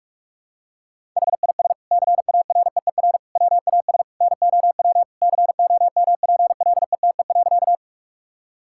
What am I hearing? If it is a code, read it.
Morse code, heard as a keyed tone, about 41 wpm, 700 Hz. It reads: HIS CAREER WAS NOW COMPLETE.
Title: Sentences from Top 1000 Words - Review Part 6 - 45wpm